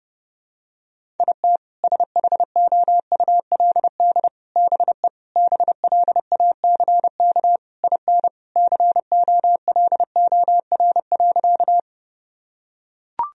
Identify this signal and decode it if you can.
Morse code, heard as a keyed tone, about 32 wpm, 700 Hz. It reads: IT SHOULD BE BLACK IN COLOR.
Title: Sentences from Top 1000 Words - Review Part 2 - 30wpm